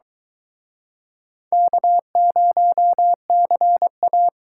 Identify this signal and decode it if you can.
Morse code, heard as a keyed tone, about 24 wpm, 700 Hz. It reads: K0CA